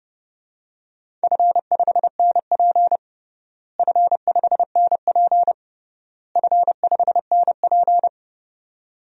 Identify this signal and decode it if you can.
Morse code, heard as a keyed tone, about 29 wpm, 700 Hz. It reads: F5NP F5NP F5NP
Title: DXCC-100 Call Signs - 4 Characters - Repeated 3x - 30wpm